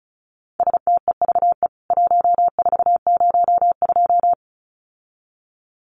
Morse code, heard as a keyed tone, about 34 wpm, 700 Hz. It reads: STEVE 1402
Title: CWT Exchange 35wpm